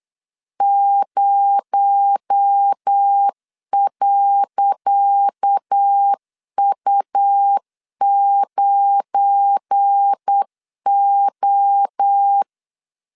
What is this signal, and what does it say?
Morse code, heard as a keyed tone, about 8 wpm, 800 Hz. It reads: 0.U9O
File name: LEZIONI / 10 SIGNS / 10C DOT 8wpm 3649